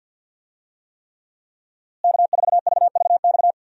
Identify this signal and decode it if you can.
Morse code, heard as a keyed tone, about 53 wpm, 700 Hz. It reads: K4VVX
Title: USA Call Signs - 5 Characters - Repeated 3x - 50wpm